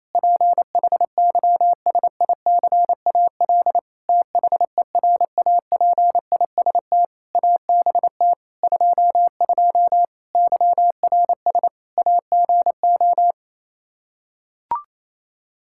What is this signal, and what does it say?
Morse code, heard as a keyed tone, about 28 wpm, 700 Hz. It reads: PHYSICAL THERAPIST ABT 22 YRS AGO
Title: Practice QSO - Exchange 3 - More Info - 28wpm